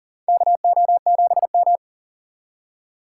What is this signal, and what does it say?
Morse code, heard as a keyed tone, about 40 wpm, 700 Hz. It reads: KO7M
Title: USA Call Signs - 4 Characters - 4x Character Spacing - 40wpm